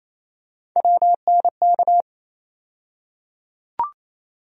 Morse code, heard as a keyed tone, about 28 wpm, 700 Hz. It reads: WNK
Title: USA Call Sign Postfixes - 3 Letters 28wpm